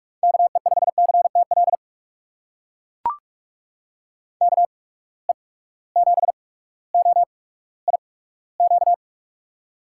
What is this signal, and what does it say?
Morse code, heard as a keyed tone, about 43 wpm, spaced as for 19 wpm, 700 Hz. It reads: KE5CTL KE7OIQ
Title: USA Call Signs - 6 Characters - 8x Character Spacing - 45wpm